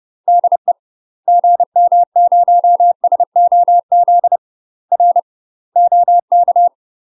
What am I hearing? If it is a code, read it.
Morse code, heard as a keyed tone, about 30 wpm, 700 Hz. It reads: DE GM0SOZ R OK
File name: UK MASTER QSOs TRAINING / 30WPM / UK MASTER QSOs 0038